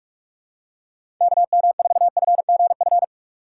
Morse code, heard as a keyed tone, about 43 wpm, 700 Hz. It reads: KM4FGF
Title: USA Call Signs - 6 Characters - 8x Character Spacing - 45wpm